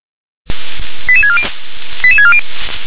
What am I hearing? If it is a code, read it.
Five-tone selective call (ZVEI), heard as quick runs of five tones, tones of about 70 ms. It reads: 88640 88640
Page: /357/SDR/ZVEI/